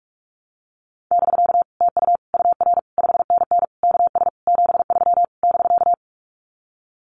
Morse code, heard as a keyed tone, about 46 wpm, 700 Hz. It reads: <BK> TU UR 5NN KS 73 <BK>
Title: Common QSO Phrases Encore 45wpm